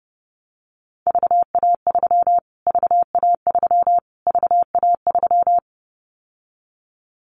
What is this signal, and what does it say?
Morse code, heard as a keyed tone, about 30 wpm, 690 Hz. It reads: VA3 VA3 VA3
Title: Callsign Prefixes for Top 25 DXCC - Repeated 3x - 30wpm